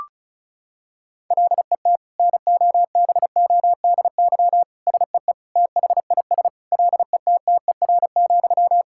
Morse code, heard as a keyed tone, about 36 wpm, 700 Hz. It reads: LET NOBODY SEE THIS LETTER,
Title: Sentences from Top 1600 Words 35wpm